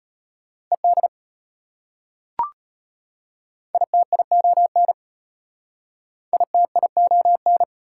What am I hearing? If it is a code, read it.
Morse code, heard as a keyed tone, about 33 wpm, 690 Hz. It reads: ED ITION ITION